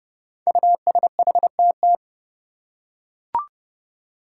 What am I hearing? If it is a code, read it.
Morse code, heard as a keyed tone, about 30 wpm, 700 Hz. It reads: USHTT
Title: Five Challenging Letter-Number Combinations 30wpm